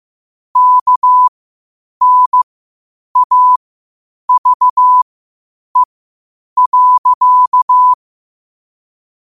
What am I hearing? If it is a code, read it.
Morse code, heard as a keyed tone, about 15 wpm, spaced as for 8 wpm, 1000 Hz. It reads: KNAVE.
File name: qotdfiles / 8wpm